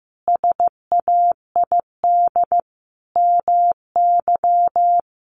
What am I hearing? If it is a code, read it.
Morse code, heard as a keyed tone, about 15 wpm, 700 Hz. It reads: SAID MY
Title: Sentences from Top 1400 Words 15wpm